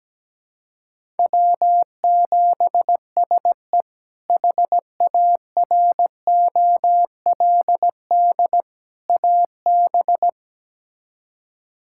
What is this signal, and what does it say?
Morse code, heard as a keyed tone, about 17 wpm, 700 Hz. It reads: W7SE HAROLD AB
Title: K1USN SST QSO Practice - 17wpm